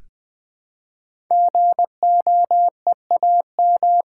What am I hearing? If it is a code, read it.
Morse code, heard as a keyed tone, about 20 wpm, 700 Hz. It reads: GOEAM
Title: Code Groups - 5 Letters - 20wpm